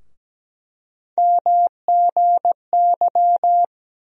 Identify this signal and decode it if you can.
Morse code, heard as a keyed tone, about 17 wpm, 700 Hz. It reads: MGY
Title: USA Call Sign Postfixes - 3 Letters 17wpm